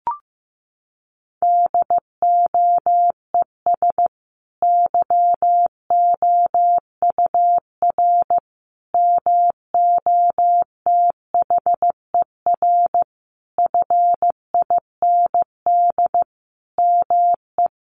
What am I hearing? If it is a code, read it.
Morse code, heard as a keyed tone, about 14 wpm, 700 Hz. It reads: DOES YOUR MOTHER FIND ME
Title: Sentences from Top 1400 Words 15wpm